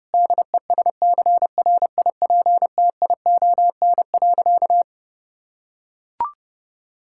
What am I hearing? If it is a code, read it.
Morse code, heard as a keyed tone, about 30 wpm, 700 Hz. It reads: DESCRIPTION.
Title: Sentences from Top 1600 Words 30wpm